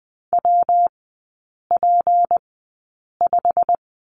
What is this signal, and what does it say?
Morse code, heard as a keyed tone, about 20 wpm, spaced as for 8 wpm, 690 Hz. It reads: WP5
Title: 212 Lesson 34e - Number 8 - Three Characters 20wpm